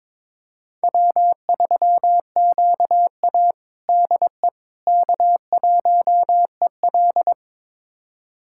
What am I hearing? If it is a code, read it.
Morse code, heard as a keyed tone, about 22 wpm, 700 Hz. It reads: W3QA DE K1EL
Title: USA Callsign DE Callsign - 4 Characters - 22wpm